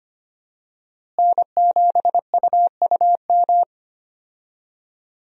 Morse code, heard as a keyed tone, about 25 wpm, 700 Hz. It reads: N7UUM
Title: USA Call Signs - 5 Characters - Repeated 3x - 25wpm